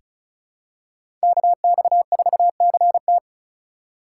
Morse code, heard as a keyed tone, about 34 wpm, 700 Hz. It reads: KX4CT
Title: USA Call Signs - 5 Characters - 3x Character Spacing - 35wpm